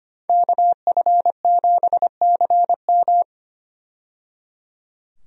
Morse code, heard as a keyed tone, about 25 wpm, 700 Hz. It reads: KF7CM
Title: USA Call Signs - 5 Characters - 25wpm